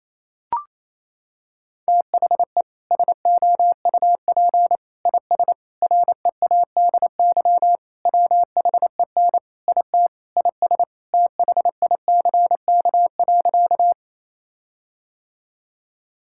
Morse code, heard as a keyed tone, about 28 wpm, 700 Hz. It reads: THE SOUP IS READY WHEN IT IS THICK.
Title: Sentences from Top 1600 Words 28wpm